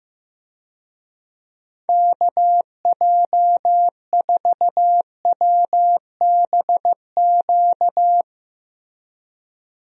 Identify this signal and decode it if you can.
Morse code, heard as a keyed tone, about 15 wpm, 700 Hz. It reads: KJ4WBQ
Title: USA Call Signs - 6 Characters - Repeated 2x - 15wpm